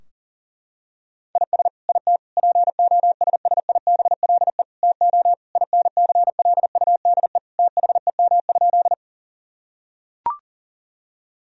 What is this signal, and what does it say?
Morse code, heard as a keyed tone, about 40 wpm, 700 Hz. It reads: IS IT POSSIBLE TO INCLUDE THEM?
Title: Sentences from Top 1000 Words - Review Part 3 - 40wpm